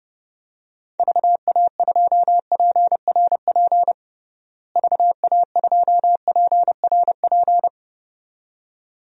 Morse code, heard as a keyed tone, about 30 wpm, 700 Hz. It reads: VA2PRP VA2PRP
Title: DXCC-100 Call Signs - 6 Characters - Repeated 2x - 30wpm